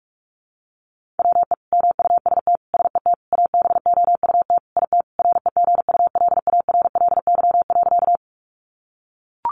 Morse code, heard as a keyed tone, about 44 wpm, 700 Hz. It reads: WE MUST SET ABOUT IT REGULARLY.
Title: Sentences from Top 1400 Words 45wpm